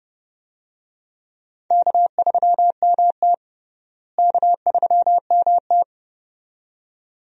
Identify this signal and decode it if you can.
Morse code, heard as a keyed tone, about 32 wpm, 700 Hz. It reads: K3MT K3MT
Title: USA Call Signs - 4 Characters - Repeated 2x - 30wpm